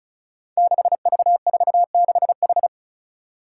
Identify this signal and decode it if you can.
Morse code, heard as a keyed tone, about 36 wpm, 700 Hz. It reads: 6V46H
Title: Five Challenging Letter-Number Combinations 35wpm